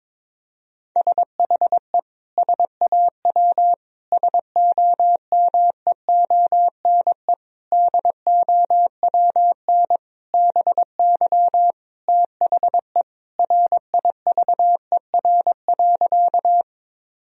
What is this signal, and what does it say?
Morse code, heard as a keyed tone, about 22 wpm, 700 Hz. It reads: SHE SAW SOMEONE DOWN BY THE RIVER.
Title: Sentences from Top 1000 Words - Review Part 4 - 22wpm